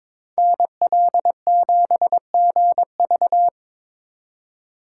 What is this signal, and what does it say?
Morse code, heard as a keyed tone, about 26 wpm, 700 Hz. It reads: NL7GV
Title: Speed Racing - USA 2x2 Call Signs 22wpm